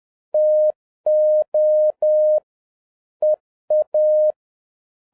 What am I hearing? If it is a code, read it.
Morse code, heard as a keyed tone, about 10 wpm, 610 Hz. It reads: TO EA